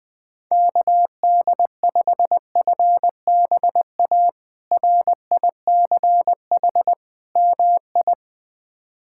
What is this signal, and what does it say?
Morse code, heard as a keyed tone, about 21 wpm, 700 Hz. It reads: KD5FBA RICH MI
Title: K1USN SST QSO Practice - 20wpm